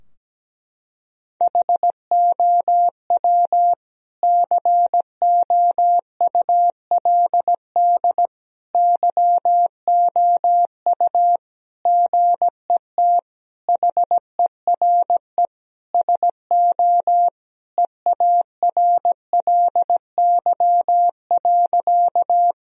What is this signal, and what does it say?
Morse code, heard as a keyed tone, about 17 wpm, 700 Hz. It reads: HOW COULD YOU GET HERE SO EARLY.
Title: Sentences from Top 1000 Words - Review Part 2 - 17wpm